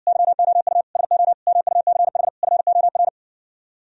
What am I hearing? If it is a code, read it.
Morse code, heard as a keyed tone, about 59 wpm, 700 Hz. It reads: YOU SO MUCH FOR